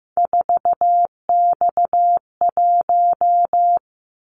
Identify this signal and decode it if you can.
Morse code, heard as a keyed tone, about 15 wpm, 700 Hz. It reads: 4X1
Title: Callsign Prefixes for Top 50 DXCC - 15wpm